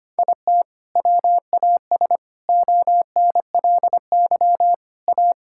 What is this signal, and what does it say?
Morse code, heard as a keyed tone, about 25 wpm, 700 Hz. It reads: IT WAS ONLY A
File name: Sets of 9 Words - 25wpm